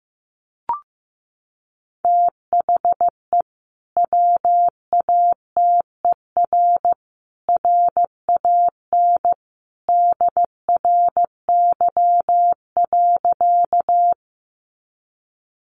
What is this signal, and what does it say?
Morse code, heard as a keyed tone, about 15 wpm, 700 Hz. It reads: THE WATER RAN DRY.
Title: Sentences from Top 500 Words 15wpm